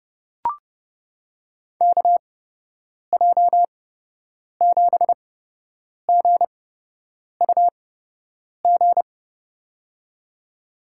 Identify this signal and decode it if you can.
Morse code, heard as a keyed tone, about 32 wpm, spaced as for 8 wpm, 700 Hz. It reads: KJ7GUG
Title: USA Call Signs - 6 Characters - 8x Character Spacing - 30wpm